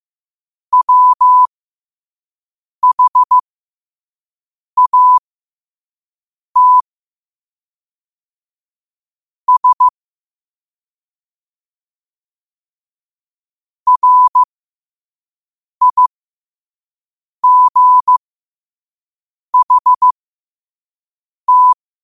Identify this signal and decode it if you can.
Morse code, heard as a keyed tone, about 15 wpm, spaced as for 5 wpm, 1000 Hz. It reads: WHAT S RIGHT
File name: 2026-02-15-5wpm